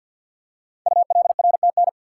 Morse code, heard as a keyed tone, about 50 wpm, 700 Hz. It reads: ULRTN
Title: Code Groups - 5 Letters - 50wpm